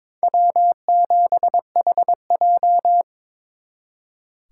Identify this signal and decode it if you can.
Morse code, heard as a keyed tone, about 26 wpm, 700 Hz. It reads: W7HJ
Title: Speed Racing - USA Call Signs 22wpm